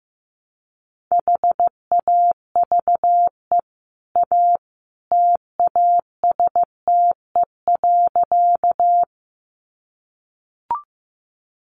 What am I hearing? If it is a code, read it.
Morse code, heard as a keyed tone, about 15 wpm, 700 Hz. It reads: HAVE A TASTE.
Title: Sentences from Top 1100 Words 15wpm